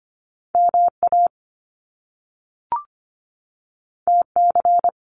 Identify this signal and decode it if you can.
Morse code, heard as a keyed tone, about 25 wpm, 700 Hz. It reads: MA TC
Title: Top Two Letter Combinations in Words 25wpm